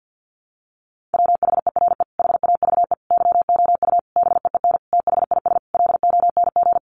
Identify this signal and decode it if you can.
Morse code, heard as a keyed tone, about 50 wpm, 700 Hz. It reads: WHERE HAVE YOU BEEN THIS LONG
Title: Sentences from Top 300 Words 50wpm